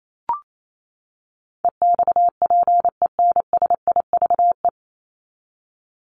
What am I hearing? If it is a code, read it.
Morse code, heard as a keyed tone, about 28 wpm, 700 Hz. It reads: EXPENSIVE
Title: Top 751-1000 Words 28wpm